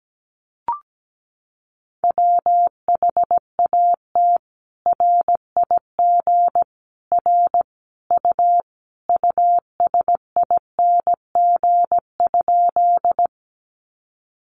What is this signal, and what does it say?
Morse code, heard as a keyed tone, about 17 wpm, 700 Hz. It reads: WHAT RIG R U USING?
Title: Common QSO Phrases 17wpm